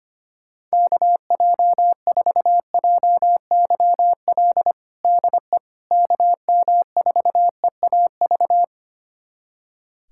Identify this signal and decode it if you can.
Morse code, heard as a keyed tone, about 25 wpm, 700 Hz. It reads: KJ4JYL DE KM4EAV
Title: USA Callsign DE Callsign 25wpm